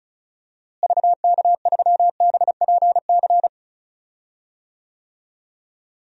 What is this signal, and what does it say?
Morse code, heard as a keyed tone, about 36 wpm, 700 Hz. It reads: VK3BPC
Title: DXCC-100 Call Signs - 6 Characters - Repeated 3x - 35wpm